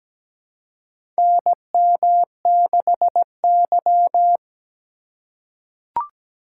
Morse code, heard as a keyed tone, about 17 wpm, 700 Hz. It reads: NM6Y
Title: USA Call Signs - 4 Characters - 4x Character Spacing - 17wpm